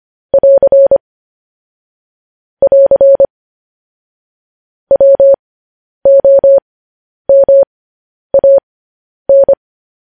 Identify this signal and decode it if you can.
Morse code, heard as a keyed tone, about 25 wpm, spaced as for 10 wpm, 550 Hz. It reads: <AR> <AR> WOMAN
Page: news-headlines-in-morse-code-at-10-wpm